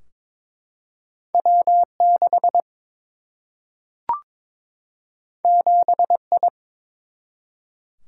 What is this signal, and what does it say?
Morse code, heard as a keyed tone, about 22 wpm, 700 Hz. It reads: W 6 7 I
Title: Single-Letter Single-Number 22wpm